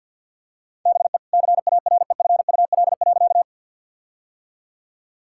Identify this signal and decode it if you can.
Morse code, heard as a keyed tone, about 50 wpm, 700 Hz. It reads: BE CAREFUL.
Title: Sentences from Top 1600 Words 50wpm